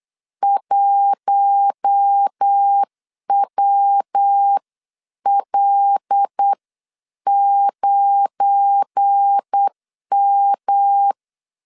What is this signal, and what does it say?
Morse code, heard as a keyed tone, about 8 wpm, spaced as for 7 wpm, 800 Hz. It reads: 1WL9M
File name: CW / DET / 1 9 1500C S A 8wpm 5637